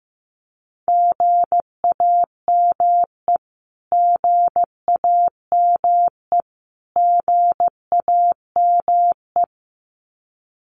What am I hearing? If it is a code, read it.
Morse code, heard as a keyed tone, about 15 wpm, 700 Hz. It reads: GAME GAME GAME
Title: Top 500 Words Repeated 15wpm